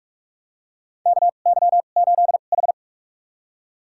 Morse code, heard as a keyed tone, about 46 wpm, spaced as for 33 wpm, 700 Hz. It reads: KY8H
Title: USA Call Signs - 4 Characters - 2x Character Spacing - 45wpm